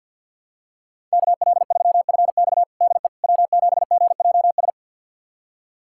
Morse code, heard as a keyed tone, about 52 wpm, 700 Hz. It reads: KL3FX DE W7GJS
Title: USA Callsign DE Callsign - 5 Characters - 50wpm